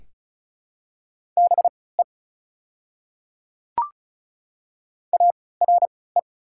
Morse code, heard as a keyed tone, about 36 wpm, spaced as for 20 wpm, 690 Hz. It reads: BE ARE
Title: 3x Character Spacing - Top 100 Words - 35wpm